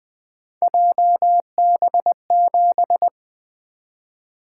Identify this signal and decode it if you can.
Morse code, heard as a keyed tone, about 20 wpm, 700 Hz. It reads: JB7
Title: Three Letter-Number Combinations - 20wpm